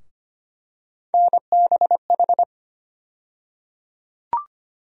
Morse code, heard as a keyed tone, about 25 wpm, 700 Hz. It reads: NBH
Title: USA Call Sign Postfixes - 3 Letters 25wpm